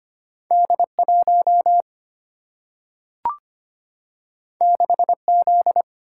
Morse code, heard as a keyed tone, about 25 wpm, 700 Hz. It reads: D 1 6 Z
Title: Single-Letter Single-Number 25wpm